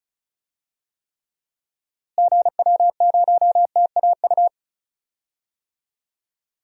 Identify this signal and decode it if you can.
Morse code, heard as a keyed tone, about 36 wpm, 700 Hz. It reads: GW0TAU
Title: DXCC-100 Call Signs - 6 Characters - Repeated 3x - 35wpm